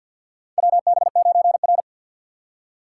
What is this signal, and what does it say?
Morse code, heard as a keyed tone, about 53 wpm, 700 Hz. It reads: WB9R
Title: USA Call Signs - 4 Characters - 6x Character Spacing - 50wpm